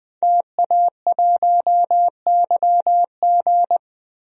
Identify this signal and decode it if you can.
Morse code, heard as a keyed tone, about 20 wpm, 700 Hz. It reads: TA1YG
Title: DXCC-100 Call Signs - 5 Characters - Repeated 3x - 20wpm